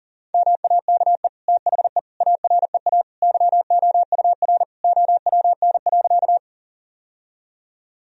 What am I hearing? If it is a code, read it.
Morse code, heard as a keyed tone, about 40 wpm, 700 Hz. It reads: MAKE THE AREA YOUR OWN.
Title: Sentences from Top 1000 Words - Review Part 2 - 40wpm